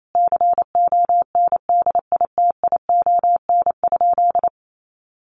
Morse code, heard as a keyed tone, about 28 wpm, 700 Hz. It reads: CONDITION?
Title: Sentences from Top 1000 Words 28wpm